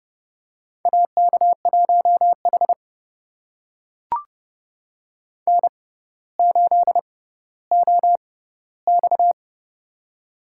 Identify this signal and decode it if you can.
Morse code, heard as a keyed tone, about 30 wpm, spaced as for 15 wpm, 700 Hz. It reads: AK1H N8OX
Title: USA Call Signs - 4 Characters - 6x Character Spacing - 30wpm